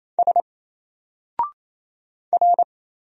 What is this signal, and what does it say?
Morse code, heard as a keyed tone, about 27 wpm, 700 Hz. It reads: S R
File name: Single-Letter-Number-28wpm